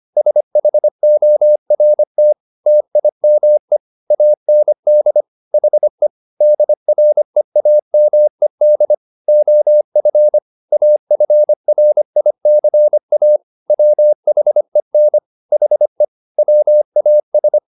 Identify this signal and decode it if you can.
Morse code, heard as a keyed tone, about 25 wpm, 600 Hz. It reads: SHORT TIME AND HE DREAMED OF AFRICA WHEN HE WAS
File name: the old man and the sea part 1 25wpm